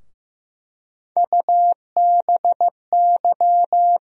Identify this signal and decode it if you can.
Morse code, heard as a keyed tone, about 15 wpm, 700 Hz. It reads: UBY
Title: USA Call Sign Postfixes - 3 Letters 15wpm